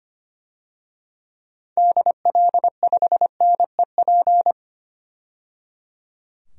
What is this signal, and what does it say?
Morse code, heard as a keyed tone, about 25 wpm, 700 Hz. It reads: DL5NEP
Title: DXCC-100 Call Signs - 6 Characters - 25wpm